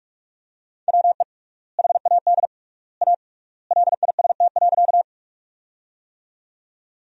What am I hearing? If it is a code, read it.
Morse code, heard as a keyed tone, about 41 wpm, 700 Hz. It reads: WE HAD A LIST.
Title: 3x Word-Spacing - Sentences from Top 400 Words - 45wpm